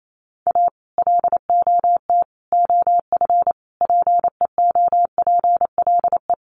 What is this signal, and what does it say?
Morse code, heard as a keyed tone, about 28 wpm, 700 Hz. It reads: A LOT OF PEOPLE
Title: Sets of 5 Words 28wpm